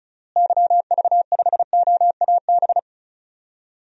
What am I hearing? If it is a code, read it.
Morse code, heard as a keyed tone, about 36 wpm, 700 Hz. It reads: YV5OAB
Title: DXCC-100 Call Signs - 6 Characters - Repeated 3x - 35wpm